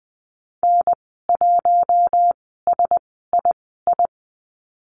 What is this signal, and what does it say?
Morse code, heard as a keyed tone, about 20 wpm, spaced as for 14 wpm, 700 Hz. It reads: N1SII
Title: 042 Lesson 8g - Number 1 - Call Signs 20wpm